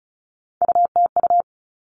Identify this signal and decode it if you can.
Morse code, heard as a keyed tone, about 34 wpm, 700 Hz. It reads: UTU
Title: All Cut Numbers - 3-Digit 35wpm